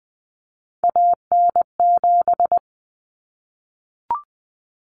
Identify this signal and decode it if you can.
Morse code, heard as a keyed tone, about 20 wpm, 700 Hz. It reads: AN7
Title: Common Cut Numbers - 3-Digit 20wpm